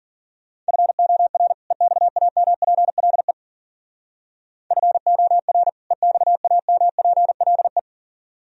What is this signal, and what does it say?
Morse code, heard as a keyed tone, about 49 wpm, 700 Hz. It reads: FOR EXAMPLE FOR EXAMPLE
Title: Speed-Racing - Sets of 2 Words 40wpm